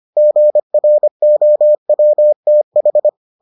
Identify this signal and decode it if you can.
Morse code, heard as a keyed tone, about 25 wpm, 600 Hz. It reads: GROWTH